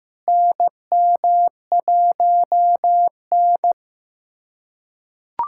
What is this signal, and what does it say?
Morse code, heard as a keyed tone, about 15 wpm, 700 Hz. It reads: NM1N
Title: USA Callsign DE Callsign - 4 Characters - 15wpm